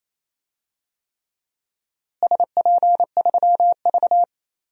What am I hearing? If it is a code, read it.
Morse code, heard as a keyed tone, about 28 wpm, 700 Hz. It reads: SP3V
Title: DXCC-100 Call Signs - 4 Characters - Repeated 3x - 28wpm